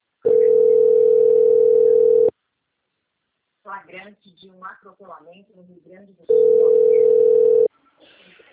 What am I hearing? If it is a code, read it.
Ringback tone (North America), 440 + 480 Hz, with speech.
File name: out- - - - - .329